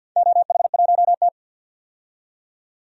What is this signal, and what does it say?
Morse code, heard as a keyed tone, about 52 wpm, 700 Hz. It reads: OH1T
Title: DXCC-100 Call Signs - 4 Characters - 50wpm